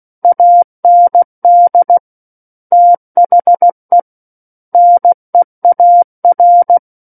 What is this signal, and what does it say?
Morse code, heard as a keyed tone, about 16 wpm, 700 Hz. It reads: AND THE NEAR
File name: Book One / 07-Chapter 7 - 16 WPM